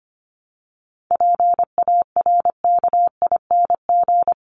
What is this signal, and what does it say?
Morse code, heard as a keyed tone, about 25 wpm, 700 Hz. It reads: PARKING
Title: Top 1501-1750 Words 25wpm